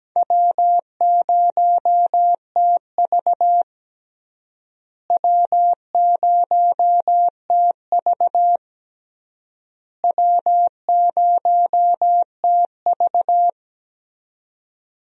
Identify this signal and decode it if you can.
Morse code, heard as a keyed tone, about 17 wpm, 700 Hz. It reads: W0TV W0TV W0TV
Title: USA Call Signs - 4 Characters - Repeated 3x - 17wpm